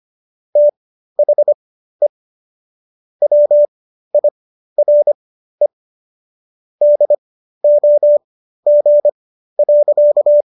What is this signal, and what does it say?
Morse code, heard as a keyed tone, about 25 wpm, spaced as for 13 wpm, 590 Hz. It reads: THE WIRE DOG.